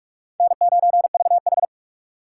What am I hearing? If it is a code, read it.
Morse code, heard as a keyed tone, about 60 wpm, spaced as for 53 wpm, 700 Hz. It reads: N9VH
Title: Speed Racing - USA 1x2 Call Signs 45wpm